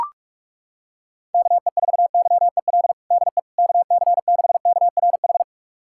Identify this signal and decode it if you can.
Morse code, heard as a keyed tone, about 41 wpm, 700 Hz. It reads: KE4YEL DE KC6KRH
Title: USA Callsign DE Callsign 45wpm